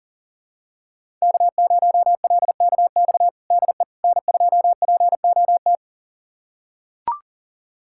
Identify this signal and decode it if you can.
Morse code, heard as a keyed tone, about 37 wpm, 700 Hz. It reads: K0LKX DE N2POT
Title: USA Callsign DE Callsign - 5 Characters - 40wpm